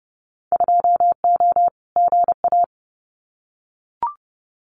Morse code, heard as a keyed tone, about 30 wpm, 700 Hz. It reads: 2O GA
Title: Winter Field Day Exchanges Encore 30wpm